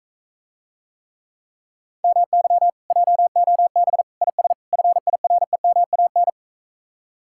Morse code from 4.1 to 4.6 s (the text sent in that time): IS